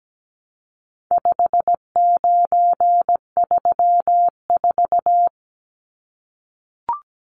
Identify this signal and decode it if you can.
Morse code, heard as a keyed tone, about 17 wpm, 700 Hz. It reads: 5934
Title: Numbers - 4-Digit 17wpm